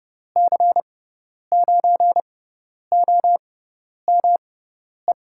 Morse code, heard as a keyed tone, about 30 wpm, spaced as for 10 wpm, 700 Hz. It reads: C9OME
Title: USA Call Signs - 6 Characters - 6x Character Spacing - 30wpm